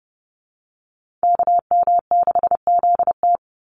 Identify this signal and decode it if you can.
Morse code, heard as a keyed tone, about 30 wpm, 700 Hz. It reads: KM6ZT
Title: Call Signs Repeated 2x - 30wpm